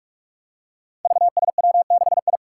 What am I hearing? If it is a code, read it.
Morse code, heard as a keyed tone, about 44 wpm, 700 Hz. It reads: VSW6I
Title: Five Challenging Letter-Number Combinations 45wpm